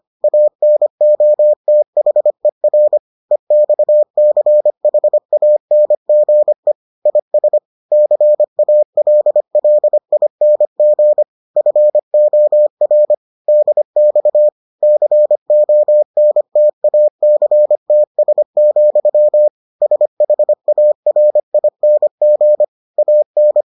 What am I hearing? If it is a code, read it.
Morse code, heard as a keyed tone, about 25 wpm, 600 Hz. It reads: ANOTHER EXCHANGE IS CALLING FOR DX CONTACTS, SHARING AN